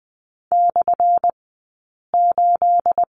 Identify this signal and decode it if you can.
Morse code, heard as a keyed tone, about 20 wpm, spaced as for 8 wpm, 700 Hz. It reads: /8
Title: 211 Lesson 34d - Number 8 - Two Characters 20wpm